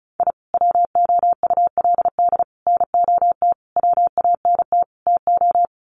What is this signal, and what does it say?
Morse code, heard as a keyed tone, about 34 wpm, 700 Hz. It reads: I WOULD NOT WANT TO
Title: Sets of 5 Words 35wpm